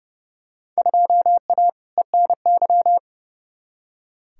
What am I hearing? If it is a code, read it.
Morse code, heard as a keyed tone, about 32 wpm, 700 Hz. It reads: 2A ENY
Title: Field Day Exchanges - 30wpm